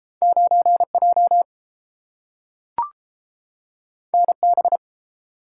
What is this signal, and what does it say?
Morse code, heard as a keyed tone, about 30 wpm, 700 Hz. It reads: 9J NB